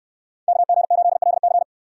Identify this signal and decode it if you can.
Morse code, heard as a keyed tone, about 66 wpm, 690 Hz. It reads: KG8LC